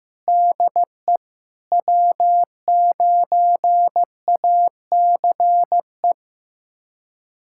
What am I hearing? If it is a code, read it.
Morse code, heard as a keyed tone, about 15 wpm, 700 Hz. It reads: DE W9ACE
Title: USA Callsign DE Callsign - 5 Characters - 15wpm